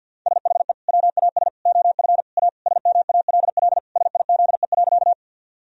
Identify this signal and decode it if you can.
Morse code, heard as a keyed tone, about 50 wpm, 700 Hz. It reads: SHE WAS OF A SMALL SIZE.
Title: Sentences from Top 1000 Words - Review Part 2 - 50wpm